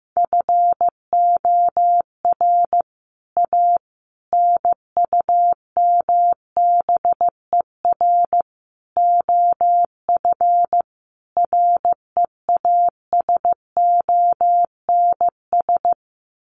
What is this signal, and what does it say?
Morse code, heard as a keyed tone, about 15 wpm, 700 Hz. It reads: FOR A NUMBER OF REASONS
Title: Sets of 5 Words Encore2 15wpm